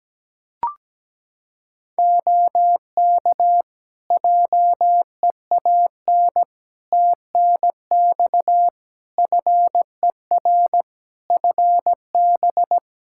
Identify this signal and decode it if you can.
Morse code, heard as a keyed tone, about 17 wpm, 700 Hz. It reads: OK JEAN TNX FER FB
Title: Practice QSO - Exchange 4 - Ending - 17wpm